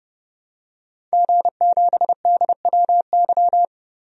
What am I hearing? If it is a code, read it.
Morse code, heard as a keyed tone, about 32 wpm, 700 Hz. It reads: G7DWY
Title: DXCC-100 Call Signs - 5 Characters - 30wpm